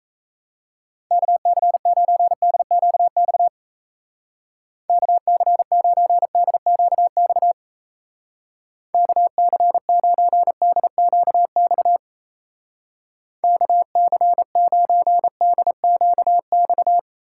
Morse code, heard as a keyed tone, about 37 wpm, 700 Hz. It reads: KC9DQX KC9DQX KC9DQX KC9DQX